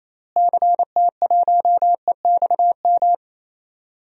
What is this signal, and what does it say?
Morse code, heard as a keyed tone, about 27 wpm, 700 Hz. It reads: CT1EXM
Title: DXCC-100 Call Signs - 6 Characters - Repeated 2x - 28wpm